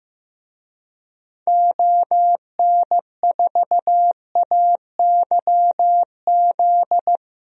Morse code, heard as a keyed tone, about 15 wpm, 700 Hz. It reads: ON4AYZ
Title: DXCC-100 Call Signs - 6 Characters - 15wpm